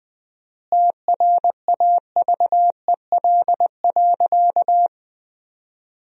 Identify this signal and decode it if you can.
Morse code, heard as a keyed tone, about 21 wpm, 700 Hz. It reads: TRAVEL.
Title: 4x Word-Spacing - Sentences from Top 400 Words - 20wpm